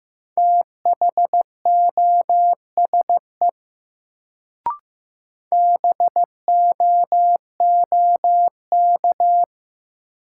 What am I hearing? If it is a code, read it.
Morse code, heard as a keyed tone, about 15 wpm, 700 Hz. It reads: THOSE BOOK
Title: Sentences from Top 300 Words - New Word Drill - 15wpm